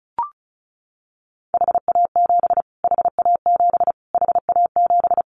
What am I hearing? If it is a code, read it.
Morse code, heard as a keyed tone, about 34 wpm, 700 Hz. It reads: HA7 HA7 HA7
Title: Callsign Prefixes for Top 25 DXCC - Repeated 3x - 35wpm